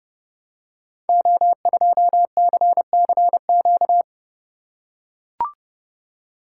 Morse code, heard as a keyed tone, about 30 wpm, 700 Hz. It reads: O2CCQ
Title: Code Groups - 5 Letter:Numbers - 30wpm